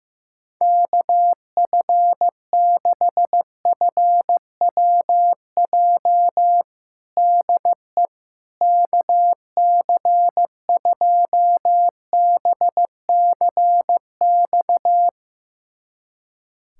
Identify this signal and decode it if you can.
Morse code, heard as a keyed tone, about 15 wpm, 700 Hz. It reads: KF6FWJ DE KC2BCX
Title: USA Callsign DE Callsign - 6 Characters - 15wpm